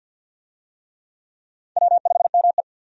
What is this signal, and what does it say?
Morse code, heard as a keyed tone, about 52 wpm, 700 Hz. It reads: W5ME
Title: USA Call Signs - 4 Characters - 50wpm